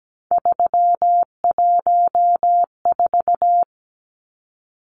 Morse code, heard as a keyed tone, about 17 wpm, 700 Hz. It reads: 314
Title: Numbers - 3-Digit 17wpm